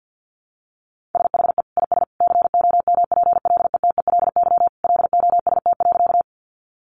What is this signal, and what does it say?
Morse code, heard as a keyed tone, about 50 wpm, 700 Hz. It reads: SHE IS COMPLETELY LOST.
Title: Additional Easy Phrases 50wpm